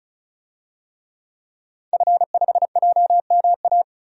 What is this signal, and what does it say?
Morse code, heard as a keyed tone, about 34 wpm, 700 Hz. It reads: F5JMA
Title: DXCC-100 Call Signs - 5 Characters - 35wpm